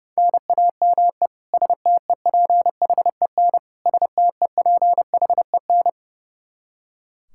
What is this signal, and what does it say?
Morse code, heard as a keyed tone, about 30 wpm, 700 Hz. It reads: NAME STEPHEN STEPHEN
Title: Practice QSO - Exchange 1 - Beginning - 30wpm